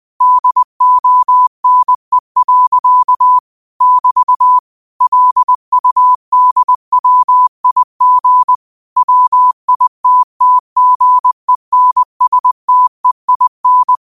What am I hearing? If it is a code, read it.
Morse code, heard as a keyed tone, about 20 wpm, 1000 Hz. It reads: DONE. <BT> LUDWIG WITTGENSTEIN
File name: qotdfiles / 20wpm